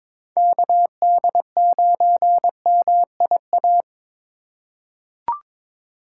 Morse code, heard as a keyed tone, about 22 wpm, 700 Hz. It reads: KD9MIA
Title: USA Call Signs - 6 Characters - 2x Character Spacing - 22wpm